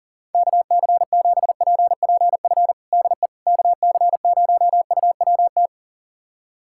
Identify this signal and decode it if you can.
Morse code, heard as a keyed tone, about 40 wpm, 700 Hz. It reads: KC7PPF DE KC0UWT
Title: USA Callsign DE Callsign - 6 Characters - 40wpm